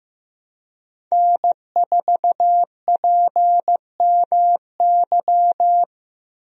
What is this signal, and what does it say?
Morse code, heard as a keyed tone, about 15 wpm, 700 Hz. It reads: N4PMY
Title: USA Call Signs - 5 Characters - Repeated 2x - 15wpm